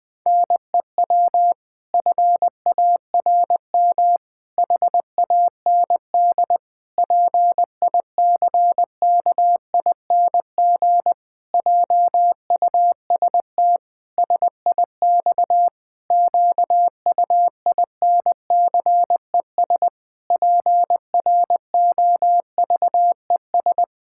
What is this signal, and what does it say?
Morse code, heard as a keyed tone, about 20 wpm, 700 Hz. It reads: NEW FARM HAND PICKING JUST SIX QUINCES PROVES
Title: Pangram Challenge v2 - 20wpm